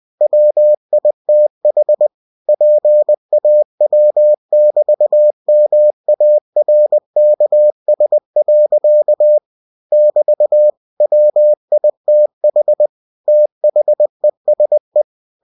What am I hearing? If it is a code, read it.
Morse code, heard as a keyed tone, about 20 wpm, 600 Hz. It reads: WITH PAW<BT>MARKS. <BT> WITH THESE
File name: cw-books / winnie-the-pooh / winnie-20wpm-page-0022